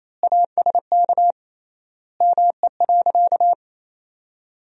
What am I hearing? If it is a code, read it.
Morse code, heard as a keyed tone, about 28 wpm, 700 Hz. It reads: ASK ME.
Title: 3x Word-Spacing - Sentences from Top 500 Words - 28wpm